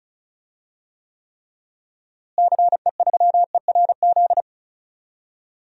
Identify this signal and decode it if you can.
Morse code, heard as a keyed tone, about 36 wpm, 700 Hz. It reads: CE3ERZ
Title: DXCC-100 Call Signs - 6 Characters - Repeated 3x - 35wpm